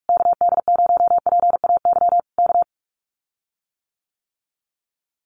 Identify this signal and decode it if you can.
Morse code, heard as a keyed tone, about 44 wpm, 700 Hz. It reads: KD0PAY K
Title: Common QSO Phrases Encore 45wpm